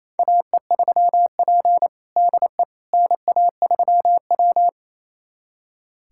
Morse code, heard as a keyed tone, about 28 wpm, 700 Hz. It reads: AE3P DE NA3W
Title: USA Callsign DE Callsign - 4 Characters - 28wpm